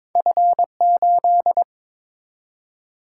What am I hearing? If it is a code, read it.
Morse code, heard as a keyed tone, about 23 wpm, 700 Hz. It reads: F8
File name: practice / 22wpm / Two Letter-Number Combinations - 22wpm